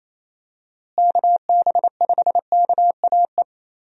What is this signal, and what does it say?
Morse code, heard as a keyed tone, about 28 wpm, 700 Hz. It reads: KB5KAE
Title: USA Call Signs - 6 Characters - Repeated 2x - 28wpm